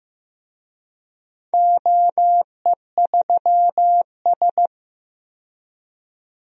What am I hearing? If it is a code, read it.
Morse code, heard as a keyed tone, about 15 wpm, 700 Hz. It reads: OE3S
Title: DXCC-100 Call Signs - 4 Characters - 15wpm